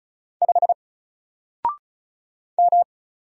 Morse code, heard as a keyed tone, about 34 wpm, 700 Hz. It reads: 5 M
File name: Single-Letter-Number-35wpm